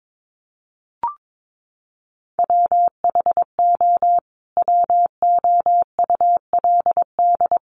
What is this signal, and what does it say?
Morse code, heard as a keyed tone, about 22 wpm, 700 Hz. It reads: WHO WOULD
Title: Sentences from Top 1300 Words 22wpm